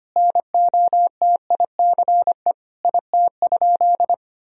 Morse code, heard as a keyed tone, about 25 wpm, 700 Hz. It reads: NOTICE IT?
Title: Sentences from Top 1000 Words - Review Part 2 - 25wpm